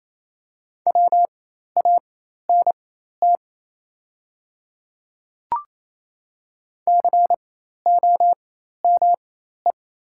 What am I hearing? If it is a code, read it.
Morse code, heard as a keyed tone, about 28 wpm, spaced as for 13 wpm, 690 Hz. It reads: WANT COME
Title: 4x Character Spacing - Top 200 Words - 28wpm